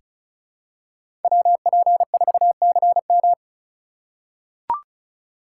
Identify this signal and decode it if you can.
Morse code, heard as a keyed tone, about 36 wpm, 700 Hz. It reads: WP4CM
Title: USA Call Signs - 4x Character Spacing - 35wpm